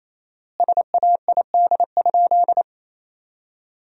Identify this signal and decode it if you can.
Morse code, heard as a keyed tone, about 28 wpm, 700 Hz. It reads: SAID?
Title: 4x Word-Spacing - Sentences from Top 100 Words - 28wpm